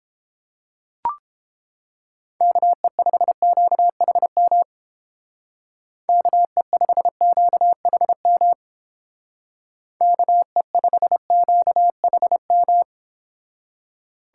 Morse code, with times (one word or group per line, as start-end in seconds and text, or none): 2.4-4.6 KE5QHM
6.1-8.5 KE5QHM
10.0-12.8 KE5QHM